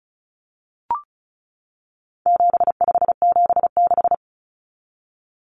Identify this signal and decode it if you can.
Morse code, heard as a keyed tone, about 34 wpm, 700 Hz. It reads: 7576
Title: Common Cut Numbers - 4-Digit 35wpm